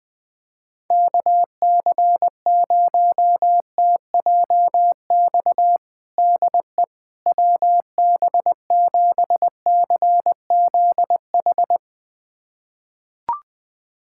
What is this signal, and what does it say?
Morse code, heard as a keyed tone, about 20 wpm, 700 Hz. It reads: KC0TJX DE WB7CZH
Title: USA Callsign DE Callsign 20wpm